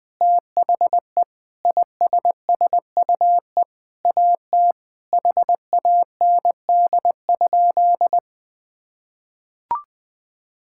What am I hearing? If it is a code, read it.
Morse code, heard as a keyed tone, about 20 wpm, 700 Hz. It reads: THE ISSUE AT HAND?
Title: Sentences with 9-Word Phrases - 20wpm